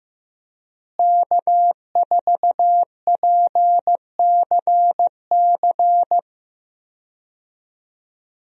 Morse code, heard as a keyed tone, about 15 wpm, 700 Hz. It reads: K4PCC
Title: USA Call Signs - 5 Characters - Repeated 3x - 15wpm